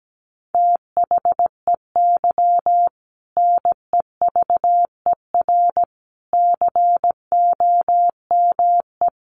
Morse code, heard as a keyed tone, about 17 wpm, 700 Hz. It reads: THEY NEVER COME
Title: Sentences from Top 300 Words 17wpm